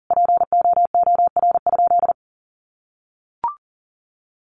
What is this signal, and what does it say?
Morse code, heard as a keyed tone, about 40 wpm, 700 Hz. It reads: POOR?
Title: Sentences from Top 1100 Words 40wpm